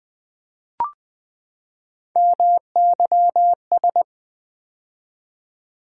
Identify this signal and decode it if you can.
Morse code, heard as a keyed tone, about 20 wpm, 700 Hz. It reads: MYS
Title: Build A Word - 3 Letters At A Time - 6-Letter Words - 20wpm